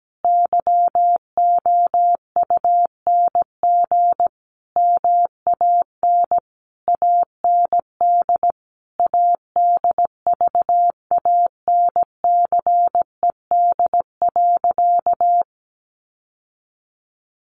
Morse code, heard as a keyed tone, about 17 wpm, 700 Hz. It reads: YOUNG MAN AND ADVANCED.
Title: Sentences from Top 1200 Words 17wpm